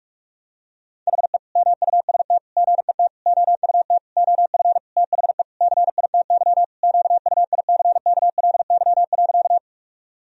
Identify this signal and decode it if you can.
Morse code, heard as a keyed tone, about 46 wpm, 700 Hz. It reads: HE MUST GET OUT OF THE CITY QUICKLY.
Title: Sentences from Top 1000 Words - Review Part 1 - 45wpm